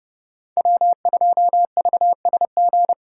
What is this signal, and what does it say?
Morse code, heard as a keyed tone, about 37 wpm, 700 Hz. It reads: W2VSG
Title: Speed Racing - USA Call Signs 30wpm